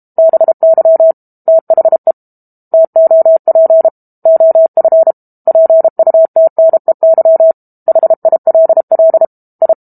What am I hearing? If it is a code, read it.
Morse code, heard as a keyed tone, about 32 wpm, 650 Hz. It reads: BY THE TOP OF PUTNEY HILL I
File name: Book Two / 06-Chapter 6 - 32 WPM